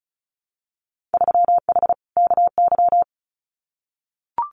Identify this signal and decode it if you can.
Morse code, heard as a keyed tone, about 34 wpm, 700 Hz. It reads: 3H KY
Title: Winter Field Day Exchanges Encore 35wpm